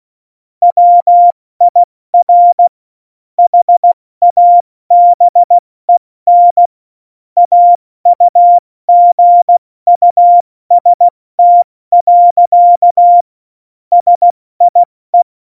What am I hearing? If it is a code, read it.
Morse code, heard as a keyed tone, about 16 wpm, 700 Hz. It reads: WIR HABEN AUGUST. SIE
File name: Morsekurs / Witze / Witz 05. Januar 2021, 70 F80